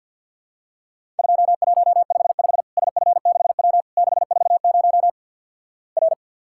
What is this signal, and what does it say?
Morse code, heard as a keyed tone, about 50 wpm, 700 Hz. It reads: 2155 SF6W 640 R
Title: Worked All Europe DX Contest - QTC Exchange - 50wpm